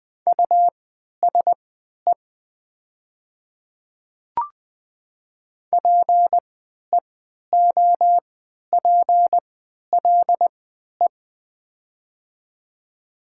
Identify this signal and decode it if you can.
Morse code, heard as a keyed tone, about 20 wpm, spaced as for 11 wpm, 700 Hz. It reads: USE PEOPLE
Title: 3x Character Spacing - Top 100 Words - 20wpm